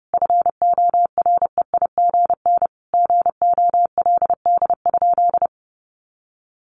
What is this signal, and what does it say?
Morse code, heard as a keyed tone, about 30 wpm, 700 Hz. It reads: FOREIGN GOLD?
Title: Sentences from Top 1500 Words 30wpm